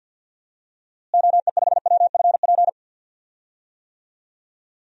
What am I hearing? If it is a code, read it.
Morse code, heard as a keyed tone, about 50 wpm, 700 Hz. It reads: OE5WFP
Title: DXCC-100 Call Signs - 6 Characters - Repeated 2x - 50wpm